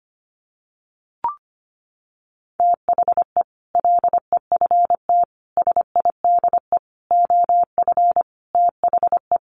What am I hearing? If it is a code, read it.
Morse code, heard as a keyed tone, about 25 wpm, 700 Hz. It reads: THE LEFT SIDE OF THE
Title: Sets of 5 Words Encore 25wpm